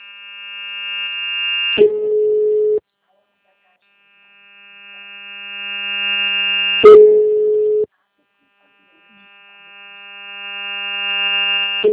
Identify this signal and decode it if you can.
Ringback tone (Europe, 425 Hz).
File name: out- -2010- - - .28494